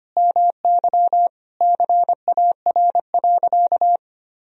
Morse code, heard as a keyed tone, about 25 wpm, 700 Hz. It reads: MY CAR.
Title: Sentences from Top 1000 Words - Review Part 1 - 25wpm